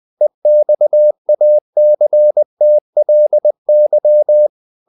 Morse code, heard as a keyed tone, about 20 wpm, 600 Hz. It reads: EXACTLY
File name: the old man and the sea / the old man and the sea part 1 20wpm